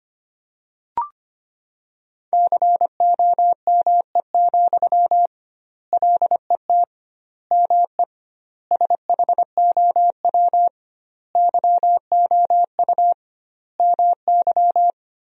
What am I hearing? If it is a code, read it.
Morse code, heard as a keyed tone, about 25 wpm, 700 Hz. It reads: COME, LET ME SHOW YOU MY
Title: 2x Word-Spacing - Sentences from Top 500 Words - 25wpm